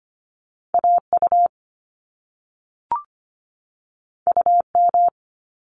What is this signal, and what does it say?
Morse code, heard as a keyed tone, about 25 wpm, 700 Hz. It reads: AU UM
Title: Top Two Letter Combinations in Words 25wpm